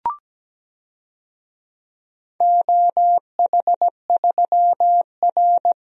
Morse code, heard as a keyed tone, about 17 wpm, 700 Hz. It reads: OH3R
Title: DXCC-100 Call Signs - 4 Characters - Repeated 2x - 17wpm